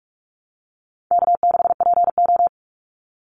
Morse code, heard as a keyed tone, about 44 wpm, 700 Hz. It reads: K6PO
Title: Easy Call Signs Repeated 2x - 45wpm